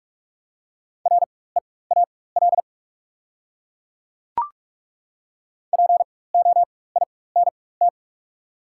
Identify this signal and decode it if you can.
Morse code, heard as a keyed tone, about 42 wpm, spaced as for 21 wpm, 690 Hz. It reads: REAL POINT
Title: 4x Character Spacing - Top 500 Words - 45wpm